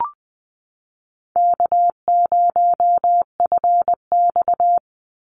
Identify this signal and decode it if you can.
Morse code, heard as a keyed tone, about 20 wpm, 700 Hz. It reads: K0FX
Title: Call Signs - 20wpm